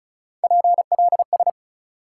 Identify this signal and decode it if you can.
Morse code, heard as a keyed tone, about 34 wpm, 700 Hz. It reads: PLS
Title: Top 180 QSO Elements 35wpm